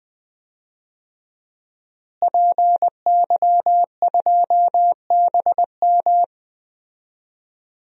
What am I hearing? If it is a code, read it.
Morse code, heard as a keyed tone, about 20 wpm, 700 Hz. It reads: PY2BM
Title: DXCC-100 Call Signs - 5 Characters - Repeated 3x - 20wpm